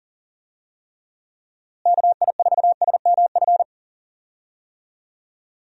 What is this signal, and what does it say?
Morse code, heard as a keyed tone, about 40 wpm, 700 Hz. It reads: KI4SMF
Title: USA Call Signs - 6 Characters - Repeated 3x - 40wpm